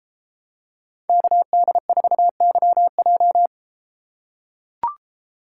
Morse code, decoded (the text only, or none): KD4YJ